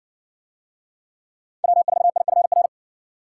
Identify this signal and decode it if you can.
Morse code, heard as a keyed tone, about 60 wpm, 700 Hz. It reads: W4IFR